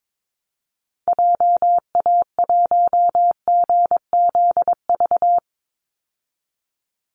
Morse code, heard as a keyed tone, about 22 wpm, 700 Hz. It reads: JA1GZV
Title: Call Signs Repeated 2x - 22wpm